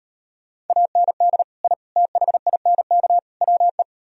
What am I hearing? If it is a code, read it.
Morse code, heard as a keyed tone, about 38 wpm, 700 Hz. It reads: AND I THINK WE